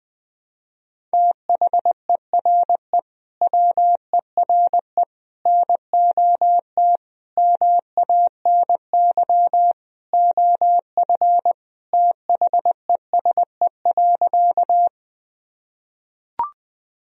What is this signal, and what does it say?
Morse code, heard as a keyed tone, about 20 wpm, 700 Hz. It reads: THERE WERE NOT MANY OF THESE.
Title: Sentences from Top 1000 Words - Review Part 1 - 20wpm